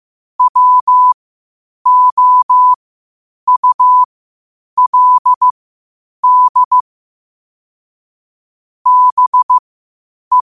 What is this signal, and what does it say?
Morse code, heard as a keyed tone, about 15 wpm, spaced as for 8 wpm, 1000 Hz. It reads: WOULD BE
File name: qotdfiles / 8wpm